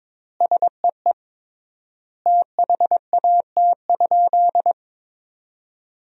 Morse code, heard as a keyed tone, about 22 wpm, 700 Hz. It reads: SEE THAT?
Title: 3x Word-Spacing - Sentences from Top 300 Words - 22wpm